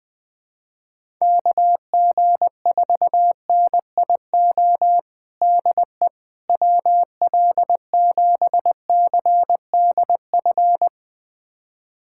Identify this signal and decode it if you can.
Morse code, heard as a keyed tone, about 21 wpm, 700 Hz. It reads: KG4NIO DE WL7CDF
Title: USA Callsign DE Callsign - 6 Characters - 20wpm